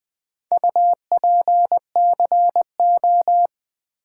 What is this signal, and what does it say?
Morse code, heard as a keyed tone, about 20 wpm, 700 Hz. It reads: UPCO
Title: Build A Word - 4 Letters At A Time - 8-Letter Words - 20wpm